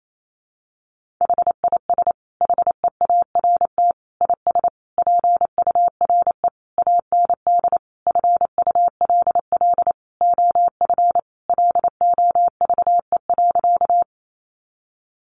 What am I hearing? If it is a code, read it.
Morse code, heard as a keyed tone, about 28 wpm, 700 Hz. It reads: HIS HEART IS PURE AND FULL OF LOVE.
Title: Sentences from Top 1500 Words 28wpm